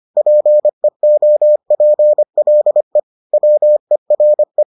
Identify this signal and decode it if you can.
Morse code, heard as a keyed tone, about 25 wpm, 600 Hz. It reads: PEOPLE WERE